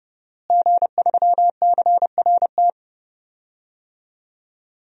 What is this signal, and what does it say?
Morse code, heard as a keyed tone, about 32 wpm, 700 Hz. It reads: G3CRT
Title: DXCC-100 Call Signs - 5 Characters - Repeated 2x - 30wpm